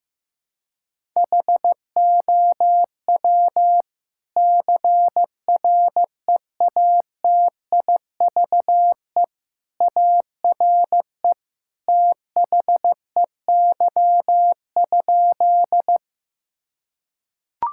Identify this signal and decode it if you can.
Morse code, heard as a keyed tone, about 15 wpm, 700 Hz. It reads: HOW CREATIVE ARE THEY?
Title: Sentences from Top 1000 Words - Review Part 5 - 15wpm